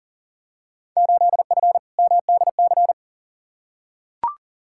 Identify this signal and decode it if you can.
Morse code, heard as a keyed tone, about 40 wpm, 700 Hz. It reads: 8F MDC
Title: Field Day Exchanges Encore - 40wpm